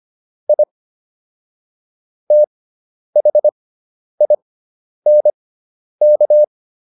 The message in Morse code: I THINK

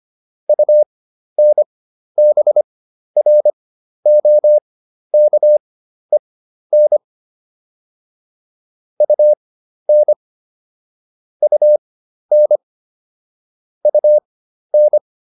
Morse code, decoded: UNBROKEN UN UN UN

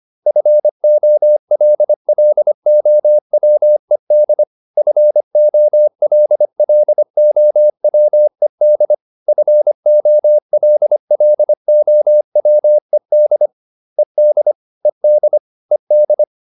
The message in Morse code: FOLLOWED FOLLOWED FOLLOWED ED ED ED

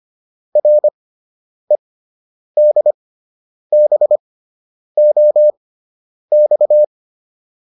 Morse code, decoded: REDBOX

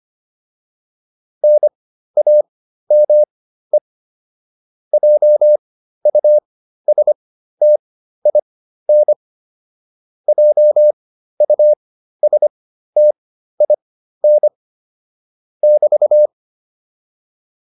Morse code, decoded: NAME JUSTIN JUSTIN <BT>

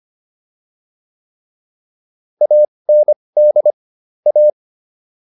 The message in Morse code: AND A

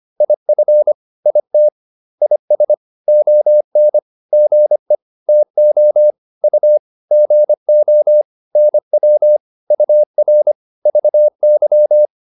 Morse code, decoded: IF IT IS ON GE TO U GO NW UR VY

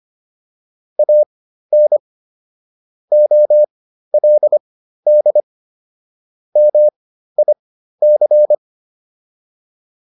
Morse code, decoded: AN OLD MIC

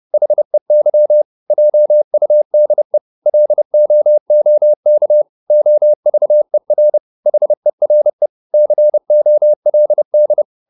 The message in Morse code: HEY JUDE LOOK OVER HERE COLD

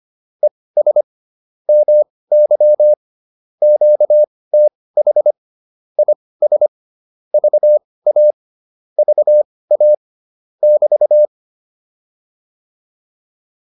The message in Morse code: ES MY QTH IS VA VA <BT>

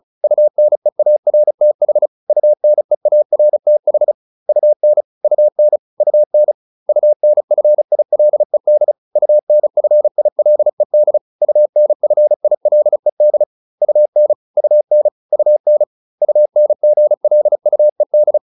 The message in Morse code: UNEARTH UNEARTH UN UN UN UNFILED UNFILED UNFILED UN UN UN UNGLUED

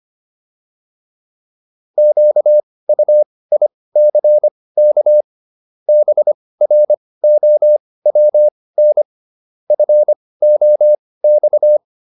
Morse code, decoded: QUICK BROWN FOX

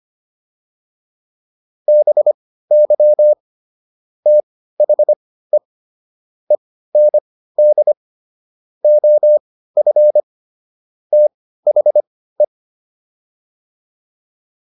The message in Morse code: BY THE END OF THE

